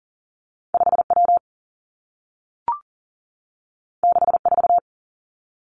Morse code, heard as a keyed tone, about 38 wpm, 700 Hz. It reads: 5 W 6 4
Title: Two Challenging Letter-Number Combinations 40wpm